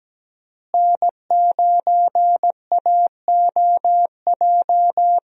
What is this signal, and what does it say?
Morse code, heard as a keyed tone, about 17 wpm, 700 Hz. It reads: N9AOJ
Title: USA Call Signs - 5 Characters - Repeated 3x - 17wpm